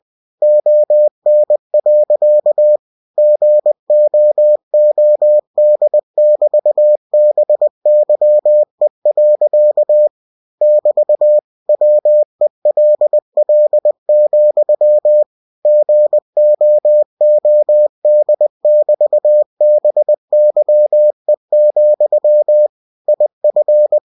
Morse code, heard as a keyed tone, about 20 wpm, 600 Hz. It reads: ON. GOOD<BT>BYE. <BT> WELL, GOOD<BT>BYE, IF
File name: cw-books / winnie-the-pooh / winnie-20wpm-page-0016